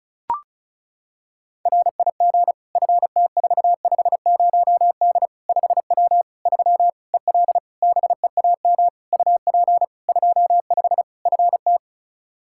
Morse code, heard as a keyed tone, about 34 wpm, 700 Hz. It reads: RIG FT450D 5W 3 EL BEAM UP 25 FT
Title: Practice QSO - Exchange 2 - Info - 35wpm